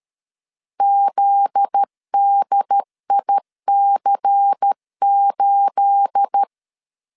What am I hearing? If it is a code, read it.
Morse code, heard as a keyed tone, about 13 wpm, 800 Hz. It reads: ZDIC8
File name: LEZIONI / 10 SIGNS / 10C DOT 12wpm 2451